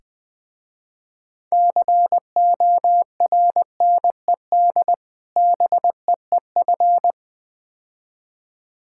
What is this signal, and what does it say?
Morse code, heard as a keyed tone, about 20 wpm, 700 Hz. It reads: CORNED BEEF
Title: Difficult Words with Hints - 20wpm